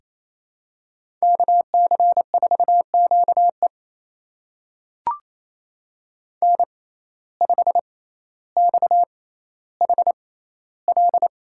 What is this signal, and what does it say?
Morse code, heard as a keyed tone, about 28 wpm, spaced as for 14 wpm, 700 Hz. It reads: KC4QE N5XHL
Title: USA Call Signs - 5 Characters - 6x Character Spacing - 28wpm